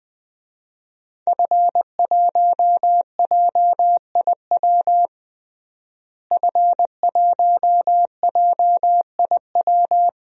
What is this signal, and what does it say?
Morse code, heard as a keyed tone, about 20 wpm, 700 Hz. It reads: F1JIW F1JIW
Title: DXCC-100 Call Signs - 5 Characters - Repeated 3x - 20wpm